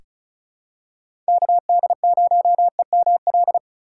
Morse code, heard as a keyed tone, about 36 wpm, 700 Hz. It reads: KD0EML
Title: USA Call Signs - 6 Characters - 4x Character Spacing - 35wpm